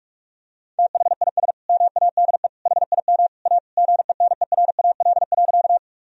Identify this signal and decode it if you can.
Morse code, heard as a keyed tone, about 46 wpm, 700 Hz. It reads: THIS MADE HIM A GENERAL.
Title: Sentences from Top 1000 Words - Review Part 3 - 45wpm